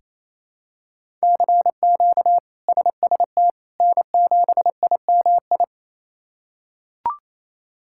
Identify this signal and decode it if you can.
Morse code, heard as a keyed tone, about 28 wpm, 700 Hz. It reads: CQ SST N7IMI
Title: K1USN SST QSO Practice - 28wpm